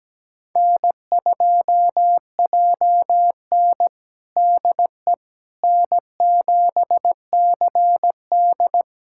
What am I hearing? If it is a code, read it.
Morse code, heard as a keyed tone, about 17 wpm, 700 Hz. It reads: N2JN DE N7CD
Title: USA Callsign DE Callsign - 4 Characters - 17wpm